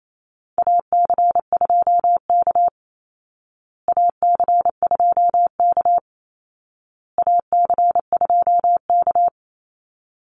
Morse code, heard as a keyed tone, about 28 wpm, 700 Hz. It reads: AC2K AC2K AC2K
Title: Call Signs Repeated 2x - 28wpm